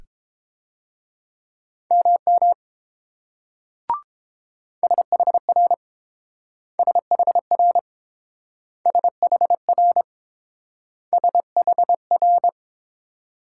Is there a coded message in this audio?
Yes: Morse code from 1.9 to 12.5 s, heard as a keyed tone, about 30 wpm, 700 Hz. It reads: MM SHR SHR SHR SHR